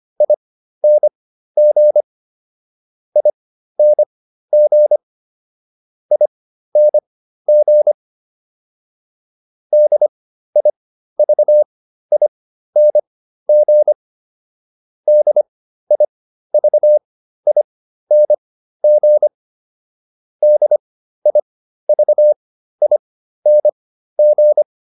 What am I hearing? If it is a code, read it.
Morse code, heard as a keyed tone, about 25 wpm, spaced as for 13 wpm, 600 Hz. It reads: ING ING ING DIVING DIVING DIVING